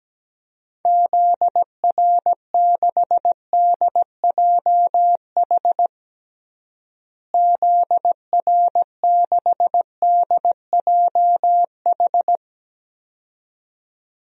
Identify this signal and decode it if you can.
Morse code, heard as a keyed tone, about 17 wpm, 700 Hz. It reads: ZR6DJH ZR6DJH
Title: DXCC-100 Call Signs - 6 Characters - Repeated 2x - 17wpm